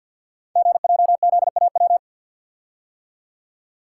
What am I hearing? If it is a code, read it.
Morse code, heard as a keyed tone, about 48 wpm, 690 Hz. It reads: GJZAW
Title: Code Groups - 5 Letters - 50wpm